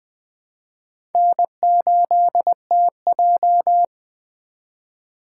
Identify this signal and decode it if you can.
Morse code, heard as a keyed tone, about 20 wpm, 700 Hz. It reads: N8TJ